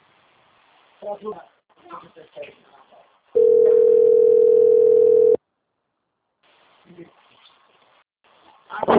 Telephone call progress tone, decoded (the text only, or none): ringback tone